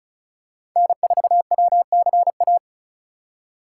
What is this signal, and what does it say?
Morse code, heard as a keyed tone, about 34 wpm, 700 Hz. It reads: N4WCA
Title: USA Call Signs - 5 Characters - 4x Character Spacing - 35wpm